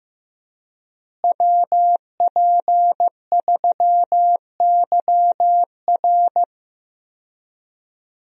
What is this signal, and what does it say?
Morse code, heard as a keyed tone, about 15 wpm, 700 Hz. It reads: WP3YR
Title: USA Call Signs - 5 Characters - Repeated 2x - 15wpm